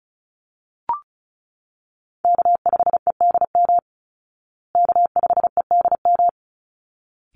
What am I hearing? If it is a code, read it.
Morse code, heard as a keyed tone, about 34 wpm, 700 Hz. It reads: K5EDM K5EDM
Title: Call Signs Repeated 3x - 35wpm